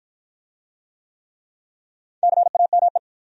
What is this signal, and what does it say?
Morse code, heard as a keyed tone, about 53 wpm, 700 Hz. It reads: CAME